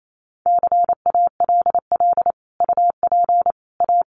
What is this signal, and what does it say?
Morse code, heard as a keyed tone, about 28 wpm, 700 Hz. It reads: CALL UP A
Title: Sentences from Top 1500 Words 28wpm